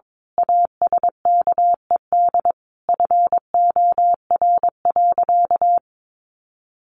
Morse code, heard as a keyed tone, about 22 wpm, 700 Hz. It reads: ASKED FOR.
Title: A Sound Of Thunder 22wpm